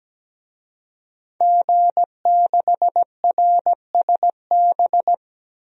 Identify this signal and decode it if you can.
Morse code, heard as a keyed tone, about 17 wpm, 700 Hz. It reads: G6RSB
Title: DXCC-100 Call Signs - 5 Characters - Repeated 3x - 17wpm